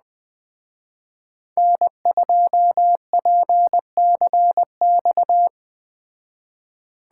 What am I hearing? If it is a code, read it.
Morse code, heard as a keyed tone, about 20 wpm, 700 Hz. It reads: N2PCX
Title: Speed Racing - USA 1x3 Call Signs 17wpm